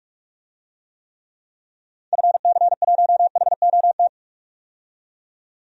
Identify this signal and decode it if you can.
Morse code, heard as a keyed tone, about 46 wpm, 700 Hz. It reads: FC1HOT
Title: DXCC-100 Call Signs - 6 Characters - 45wpm